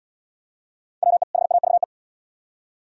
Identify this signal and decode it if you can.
Morse code, heard as a keyed tone, about 67 wpm, 700 Hz. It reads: WE HAVE